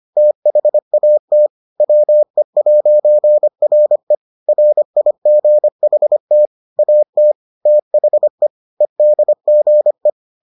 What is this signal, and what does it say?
Morse code, heard as a keyed tone, about 25 wpm, 600 Hz. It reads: THAT WE'RE RIGHT AT THE EDGE